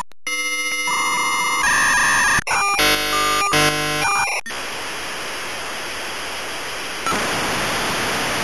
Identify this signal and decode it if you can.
Dial-up modem handshake.
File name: Modem Sounds / sm56 aol v90